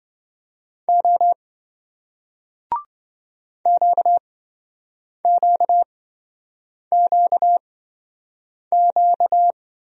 Morse code, heard as a keyed tone, about 27 wpm, spaced as for 7 wpm, 700 Hz. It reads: O QQQQ